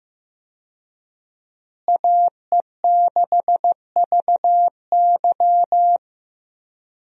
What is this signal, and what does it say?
Morse code, heard as a keyed tone, about 15 wpm, 700 Hz. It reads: AE6VY
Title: USA Call Signs - 5 Characters - Repeated 3x - 15wpm